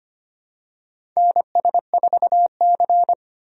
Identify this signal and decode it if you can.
Morse code, heard as a keyed tone, about 25 wpm, 700 Hz. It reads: NS4C
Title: USA Call Signs - 4 Characters - 25wpm